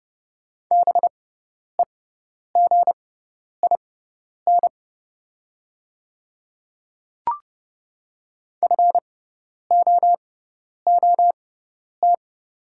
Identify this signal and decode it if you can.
Morse code, heard as a keyed tone, about 32 wpm, spaced as for 10 wpm, 690 Hz. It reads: BEGIN FOOT
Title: 6x Character Spacing - Top 500 Words - 30wpm